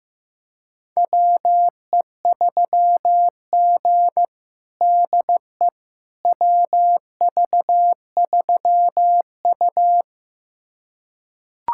Morse code, heard as a keyed tone, about 15 wpm, 700 Hz. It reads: WE3G DE WV3U
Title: USA Callsign DE Callsign - 4 Characters - 15wpm